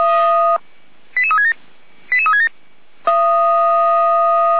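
Five-tone selective call (ZVEI), heard as quick runs of five tones, tones of about 80 ms. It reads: 88367 88367
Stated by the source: Five-tone ZVEI selective call 88199 received on SDR, from the server of feuerwehr-orlamuende (fire brigade Orlamünde).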